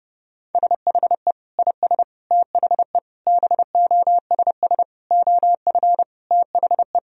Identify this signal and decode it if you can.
Morse code, heard as a keyed tone, about 30 wpm, 700 Hz. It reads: SHE IS THE BOSS OF THE
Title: Sentences from Top 1000 Words - Review Part 4 - 30wpm